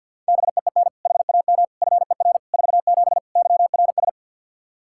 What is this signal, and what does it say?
Morse code, heard as a keyed tone, about 47 wpm, 700 Hz. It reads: BEEN HAM FER 47 YRS
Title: Practice QSO - Exchange 3 - More Info - 50wpm